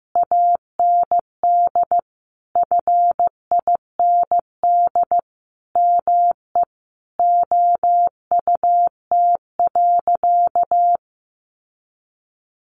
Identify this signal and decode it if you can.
Morse code, heard as a keyed tone, about 15 wpm, 700 Hz. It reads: AND FIND ME OUT.
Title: Sentences from Top 500 Words 15wpm